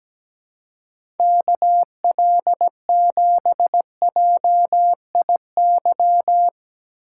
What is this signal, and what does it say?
Morse code, heard as a keyed tone, about 17 wpm, 700 Hz. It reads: KL7JIY
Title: USA Call Signs - 6 Characters - 4x Character Spacing - 17wpm